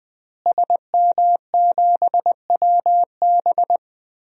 Morse code, heard as a keyed tone, about 20 wpm, 700 Hz. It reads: SM7WB
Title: DXCC-100 Call Signs - 5 Characters - Repeated 2x - 20wpm